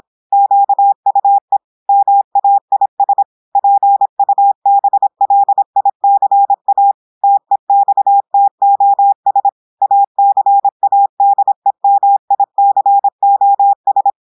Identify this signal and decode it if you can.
Morse code, heard as a keyed tone, about 26 wpm, 800 Hz. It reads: QUE MAIS PUBLICA TEXTOS ACADEMICOS
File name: audios / portugais / portugais texte en clair 1 vitesse 26 mots minutes